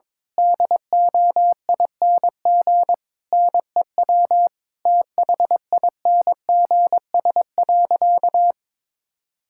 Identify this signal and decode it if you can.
Morse code, heard as a keyed tone, about 22 wpm, 700 Hz. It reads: DOING NEW THINGS.
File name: Difficult Sentences with Hints - 22wpm